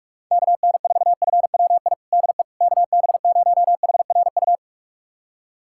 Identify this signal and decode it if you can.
Morse code, heard as a keyed tone, about 46 wpm, 700 Hz. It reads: KN4FWI DE KB0HRU
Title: USA Callsign DE Callsign - 6 Characters - 45wpm